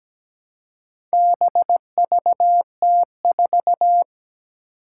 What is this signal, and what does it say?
Morse code, heard as a keyed tone, about 17 wpm, 700 Hz. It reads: BVT4
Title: Four Challenging Letter-Number Combinations 17wpm